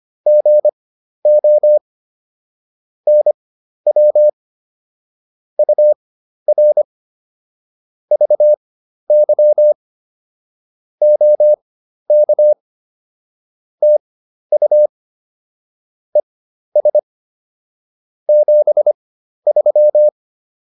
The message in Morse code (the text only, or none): GO NW UR VY OK TU ES 73